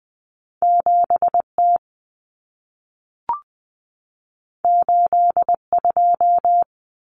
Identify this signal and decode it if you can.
Morse code, heard as a keyed tone, about 20 wpm, 700 Hz. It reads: 7T 82
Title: Common Cut Numbers - 2-Digit 20wpm